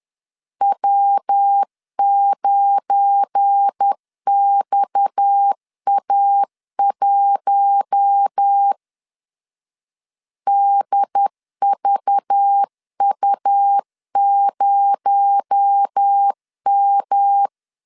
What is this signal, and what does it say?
Morse code, heard as a keyed tone, about 10 wpm, 800 Hz. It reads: W9XA1 DVU0M
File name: LEZIONI / 7 GF9Z / 7 GF9 10wpm 2747